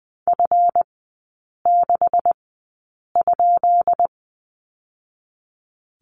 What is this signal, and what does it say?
Morse code, heard as a keyed tone, about 20 wpm, 700 Hz. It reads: F 6 ?
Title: 218 Lesson 35e - Number 0 - Three Characters 20wpm